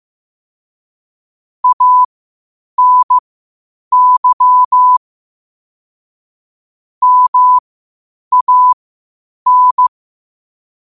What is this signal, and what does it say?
Morse code, heard as a keyed tone, about 15 wpm, spaced as for 8 wpm, 1000 Hz. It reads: ANY MAN
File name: qotdfiles / 8wpm